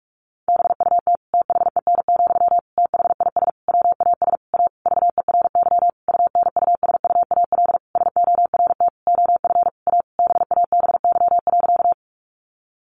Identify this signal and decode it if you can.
Morse code, heard as a keyed tone, about 44 wpm, 700 Hz. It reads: BUT THEN, THIS WAS A VERY UNUSUAL SORT OF A BABY.
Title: Sentences from Top 1400 Words 45wpm